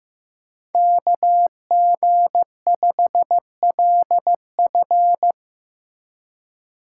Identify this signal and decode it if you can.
Morse code, heard as a keyed tone, about 17 wpm, 700 Hz. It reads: KG5LF
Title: Speed Racing - USA 2x2 Call Signs 15wpm